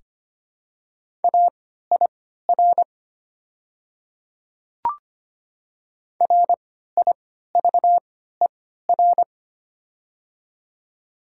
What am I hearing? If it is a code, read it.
Morse code, heard as a keyed tone, about 25 wpm, spaced as for 14 wpm, 700 Hz. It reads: AIR RIVER
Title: 3x Character Spacing - Top 300 Words - 25wpm